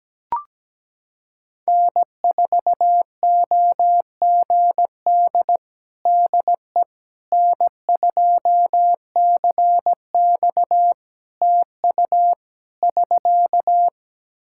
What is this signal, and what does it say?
Morse code, heard as a keyed tone, about 17 wpm, 700 Hz. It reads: N4OGD DE N2CX TU <SK>
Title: Practice QSO - Exchange 4 - Ending - 17wpm